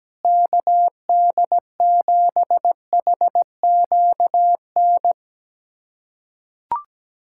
Morse code, heard as a keyed tone, about 17 wpm, 700 Hz. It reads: KD7HQN
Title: USA Callsign DE Callsign 17wpm